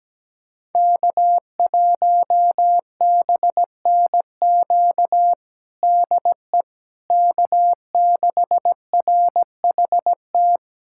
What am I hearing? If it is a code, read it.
Morse code, heard as a keyed tone, about 17 wpm, 700 Hz. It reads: K1BNQ DE K6RHT
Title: K1USN SST QSO Practice 17wpm